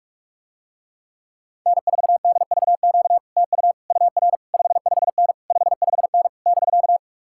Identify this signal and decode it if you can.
Morse code, heard as a keyed tone, about 50 wpm, 700 Hz. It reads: N4DVQ TU UR 55N 55N <BK>